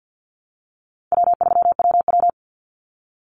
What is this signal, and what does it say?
Morse code, heard as a keyed tone, about 50 wpm, 700 Hz. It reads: W3WW
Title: Easy Call Signs - 50wpm